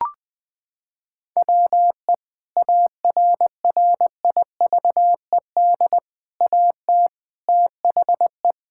Morse code, heard as a keyed tone, about 20 wpm, 700 Hz. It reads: WE ARRIVED AT THE
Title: Sentences with 3-Word Phrases - 20wpm